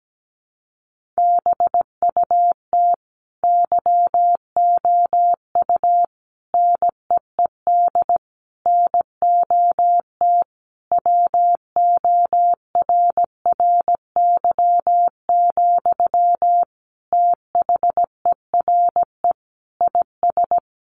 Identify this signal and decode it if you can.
Morse code, heard as a keyed tone, about 17 wpm, 700 Hz. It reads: BUT YOU NEED NOT WORRY, THERE IS
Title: Sentences from Top 1300 Words 17wpm